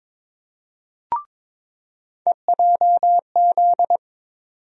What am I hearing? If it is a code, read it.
Morse code, heard as a keyed tone, about 22 wpm, 700 Hz. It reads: EJZ
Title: USA Call Sign Postfixes - 3 Letters 22wpm